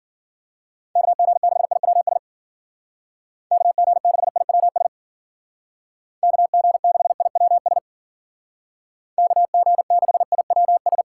Morse code, heard as a keyed tone, about 50 wpm, 700 Hz. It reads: KG6IWS KG6IWS KG6IWS KG6IWS